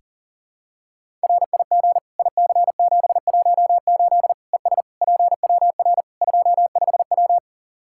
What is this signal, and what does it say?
Morse code, heard as a keyed tone, about 40 wpm, 700 Hz. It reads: RIG IC718 ES PWR 25W
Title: Practice QSO - Exchange 2 - Info - 40wpm